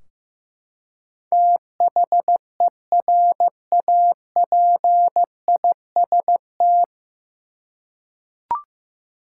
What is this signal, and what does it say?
Morse code, heard as a keyed tone, about 15 wpm, 700 Hz. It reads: THERAPIST
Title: Practice QSO - Exchange 3 - More Info - 15wpm